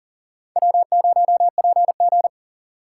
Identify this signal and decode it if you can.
Morse code, heard as a keyed tone, about 38 wpm, 700 Hz. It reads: W0PG
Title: USA Call Signs - 2x Character Spacing - 40wpm